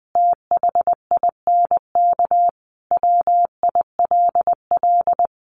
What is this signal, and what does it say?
Morse code, heard as a keyed tone, about 19 wpm, 700 Hz. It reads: THINK WILL
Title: Sentences from Top 1500 Words 20wpm